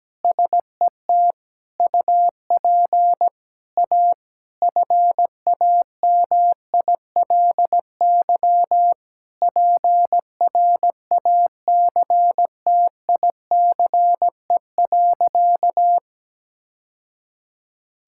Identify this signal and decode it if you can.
Morse code, heard as a keyed tone, about 17 wpm, 700 Hz. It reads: SET UP A FAMILY PRACTICE.
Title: Sentences from Top 1000 Words - Review Part 5 - 17wpm